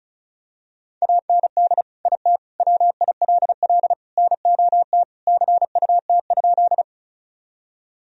Morse code, heard as a keyed tone, about 36 wpm, 700 Hz. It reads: AND IT WILL NOT CUT?
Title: Sentences from Top 1000 Words - Review Part 1 - 35wpm